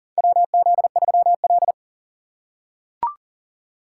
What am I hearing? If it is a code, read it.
Morse code, heard as a keyed tone, about 40 wpm, 700 Hz. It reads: WZ3L
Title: USA Call Signs - 4 Characters - 2x Character Spacing - 40wpm